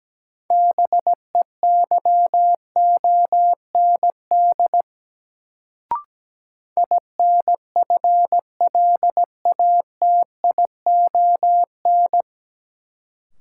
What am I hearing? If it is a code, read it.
Morse code, heard as a keyed tone, about 17 wpm, 700 Hz. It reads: BEYOND INFLATION
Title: Sentences from Top 1100 Words - New Word Drill Alt - 17wpm